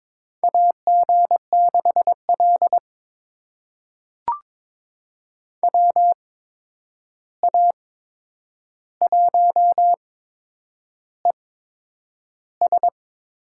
Morse code, heard as a keyed tone, about 22 wpm, spaced as for 9 wpm, 700 Hz. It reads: AG6L WA1ES
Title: USA Call Signs - 8x Character Spacing - 22wpm